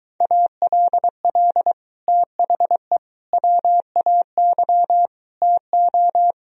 Morse code, heard as a keyed tone, about 23 wpm, 700 Hz. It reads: ALL THE WAY TO